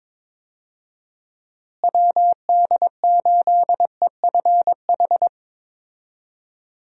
Morse code, heard as a keyed tone, about 22 wpm, 700 Hz. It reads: WD8EFH
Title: USA Call Signs - 6 Characters - Repeated 3x - 22wpm